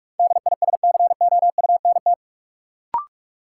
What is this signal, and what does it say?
Morse code, heard as a keyed tone, about 46 wpm, 700 Hz. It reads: DISCOUNT
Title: Sentences from Top 1200 Words - New Word Drill - 45wpm